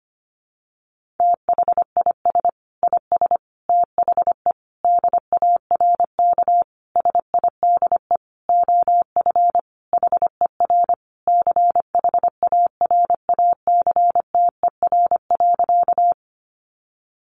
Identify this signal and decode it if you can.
Morse code, heard as a keyed tone, about 25 wpm, 700 Hz. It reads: THIS IS THE DARK SIDE OF HER CHARACTER.
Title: Sentences from Top 1200 Words 25wpm